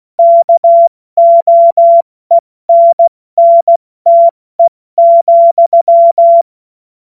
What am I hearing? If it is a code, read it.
Morse code, heard as a keyed tone, about 16 wpm, 680 Hz. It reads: KOENNTE,
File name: Morsekurs / Witze / Witz 30. August 2022 70 F80